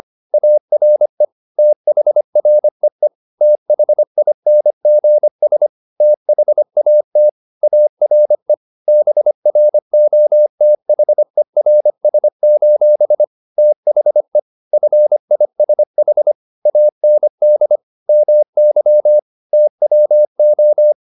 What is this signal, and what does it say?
Morse code, heard as a keyed tone, about 25 wpm, 600 Hz. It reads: ARE THREE THINGS THAT ARE BROTHERS: THE FISH AND MY TWO
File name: the old man and the sea part 1 25wpm